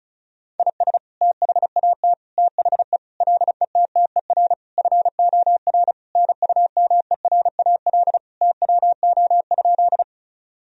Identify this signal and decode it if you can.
Morse code, heard as a keyed tone, about 34 wpm, 700 Hz. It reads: IS THAT THE LETTER FOR NUMERAL TWO?
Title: Sentences from Top 1000 Words - Review Part 2 - 35wpm